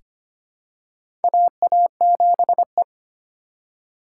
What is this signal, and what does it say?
Morse code, heard as a keyed tone, about 25 wpm, 700 Hz. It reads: AA7E
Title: USA Call Signs - 4 Characters - 2x Character Spacing - 25wpm